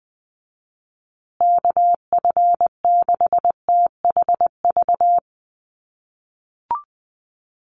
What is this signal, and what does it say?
Morse code, heard as a keyed tone, about 20 wpm, 700 Hz. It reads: KF6THV
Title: USA Call Signs Repeated 2x 20wpm